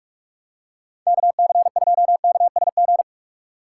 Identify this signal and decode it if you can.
Morse code, heard as a keyed tone, about 46 wpm, 700 Hz. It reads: KC2KSG
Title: USA Call Signs - 6 Characters - 8x Character Spacing - 45wpm